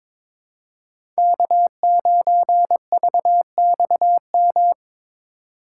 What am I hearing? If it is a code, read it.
Morse code, heard as a keyed tone, about 22 wpm, 700 Hz. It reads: K9VXM
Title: USA Call Signs - 5 Characters - 3x Character Spacing - 22wpm